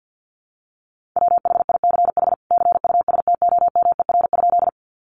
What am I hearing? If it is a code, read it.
Morse code, heard as a keyed tone, about 50 wpm, 700 Hz. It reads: WHICH CUSTOMER?
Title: Sentences from Top 1100 Words 50wpm